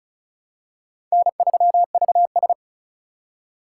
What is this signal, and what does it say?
Morse code, heard as a keyed tone, about 36 wpm, 700 Hz. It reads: N3VS
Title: USA Call Signs - 2x Character Spacing - 35wpm